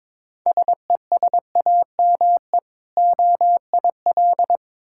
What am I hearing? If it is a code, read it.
Morse code, heard as a keyed tone, about 22 wpm, 690 Hz. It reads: SESAME OIL
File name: Difficult Words with Hints - 22wpm